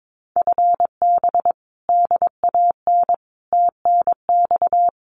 Morse code, heard as a keyed tone, about 22 wpm, 700 Hz. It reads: FB DAN TNX
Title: Common QSO Phrases Encore 22wpm